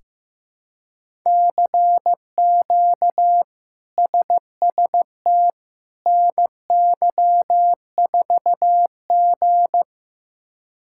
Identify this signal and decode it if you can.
Morse code, heard as a keyed tone, about 15 wpm, 700 Hz. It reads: CQ SST NY4G
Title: K1USN SST QSO Practice - 15wpm